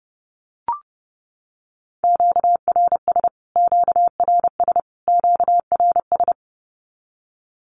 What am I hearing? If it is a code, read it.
Morse code, heard as a keyed tone, about 30 wpm, 700 Hz. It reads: QRS QRS QRS
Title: Q-Codes Repeated 3x 30wpm